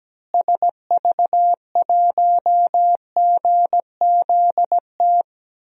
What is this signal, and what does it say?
Morse code, heard as a keyed tone, about 17 wpm, 700 Hz. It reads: SV1GZT
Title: DXCC-100 Call Signs - 6 Characters - Repeated 3x - 17wpm